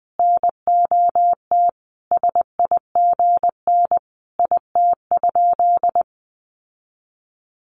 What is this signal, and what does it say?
Morse code, heard as a keyed tone, about 20 wpm, 700 Hz. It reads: NOT SIGN IT?
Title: Sentences from Top 1000 Words 20wpm